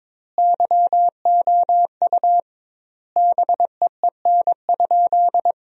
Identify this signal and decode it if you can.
Morse code, heard as a keyed tone, about 22 wpm, 700 Hz. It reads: YOU BEEN?
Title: 2x Word-Spacing - Sentences from Top 400 Words - 22wpm